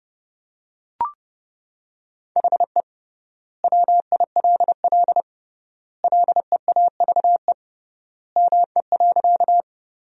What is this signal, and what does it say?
Morse code, heard as a keyed tone, about 30 wpm, 700 Hz. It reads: HE WILL LEAVE ME.
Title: 3x Word-Spacing - Sentences from Top 400 Words - 30wpm